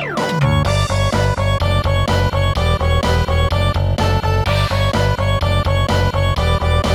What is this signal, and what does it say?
There is no code.